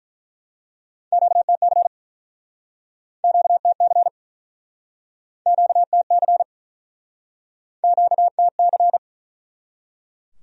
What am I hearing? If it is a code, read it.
Morse code, heard as a keyed tone, about 47 wpm, 700 Hz. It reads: QTC QTC QTC QTC